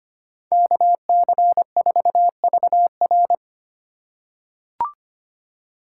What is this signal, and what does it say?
Morse code, heard as a keyed tone, about 25 wpm, 700 Hz. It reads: KC4VR
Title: USA Call Signs - 6x Character Spacing - 25wpm